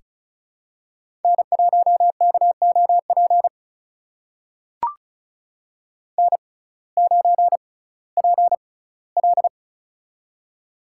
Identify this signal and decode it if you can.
Morse code, heard as a keyed tone, about 36 wpm, spaced as for 19 wpm, 700 Hz. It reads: N1KOP N9PL
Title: USA Call Signs - 6x Character Spacing - 35wpm